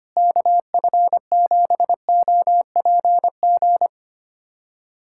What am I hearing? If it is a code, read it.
Morse code, heard as a keyed tone, about 25 wpm, 700 Hz. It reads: KF7OPG
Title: USA Callsign DE Callsign - 6 Characters - 25wpm